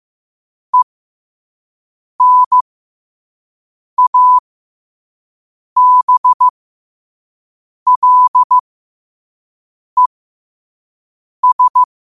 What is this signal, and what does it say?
Morse code, heard as a keyed tone, about 15 wpm, spaced as for 5 wpm, 1000 Hz. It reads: ENABLES
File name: qotdfiles / 5wpm